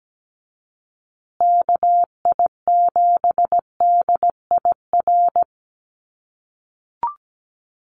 Morse code, heard as a keyed tone, about 17 wpm, 700 Hz. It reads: KI7DIR
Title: USA Call Signs Repeated 3x 17wpm